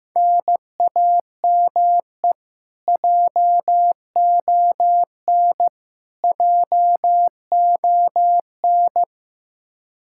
Morse code, heard as a keyed tone, about 15 wpm, 700 Hz. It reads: NAME JON JON
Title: K1USN SST QSO Practice 15wpm